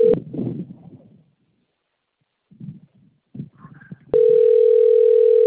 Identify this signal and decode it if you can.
Ringback tone (North America), 440 + 480 Hz.